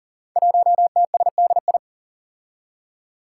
Morse code, heard as a keyed tone, about 37 wpm, 700 Hz. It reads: 1TSDI
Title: Five Challenging Letter-Number Combinations 40wpm